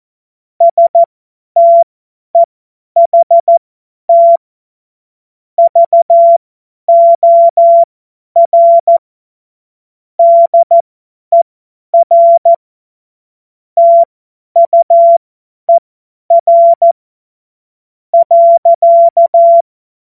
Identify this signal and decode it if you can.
Morse code, heard as a keyed tone, about 14 wpm, spaced as for 10 wpm, 680 Hz. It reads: STEHT VOR DER TUER .
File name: Morsekurs / Witze / Witz 09. Februar 2021, 50 F70